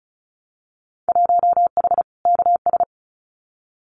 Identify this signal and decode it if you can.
Morse code, heard as a keyed tone, about 36 wpm, 700 Hz. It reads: 1H KS
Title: Winter Field Day Exchanges 35wpm